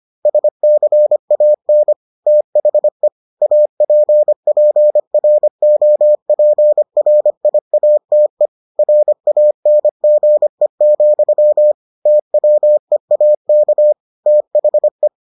Morse code, heard as a keyed tone, about 25 wpm, 600 Hz. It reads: SCAN THE APPROPRIATE RANGE, TWEAK THE